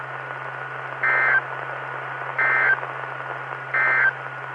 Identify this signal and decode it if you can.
Emergency Alert System (SAME) tones, heard as harsh buzzing bursts. It is heard from 0.9 to 4.2 s.
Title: Saline, AK SKYWARN Repeater System - W5RHS